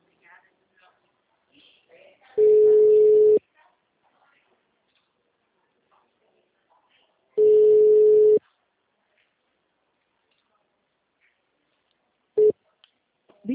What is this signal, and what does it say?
Ringback tone (Europe, 425 Hz), with speech.